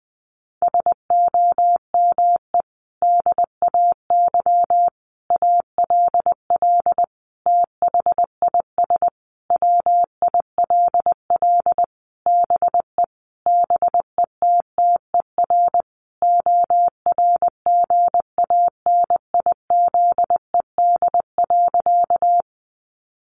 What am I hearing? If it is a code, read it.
Morse code, heard as a keyed tone, about 20 wpm, 700 Hz. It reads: SOME DAY ALL THIS WILL BE BETTER ORGANIZED.
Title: Sentences from Top 1300 Words 20wpm